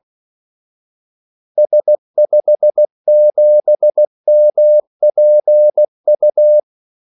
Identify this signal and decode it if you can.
Morse code, heard as a keyed tone, about 16 wpm, 600 Hz. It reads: S57MPU
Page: Kaldesignaler 16 wpm